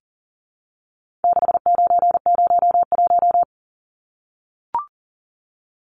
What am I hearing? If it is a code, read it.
Morse code, heard as a keyed tone, about 40 wpm, 700 Hz. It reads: 6901
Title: Numbers - 4-Digit 40wpm